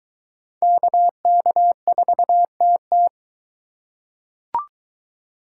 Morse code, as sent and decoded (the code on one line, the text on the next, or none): -.- -.- ....- - -
KK4TT